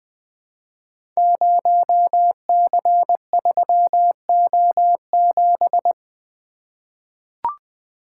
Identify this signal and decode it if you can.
Morse code, heard as a keyed tone, about 20 wpm, 700 Hz. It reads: 0C3O7
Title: Code Groups - 5 Letter:Numbers - 20wpm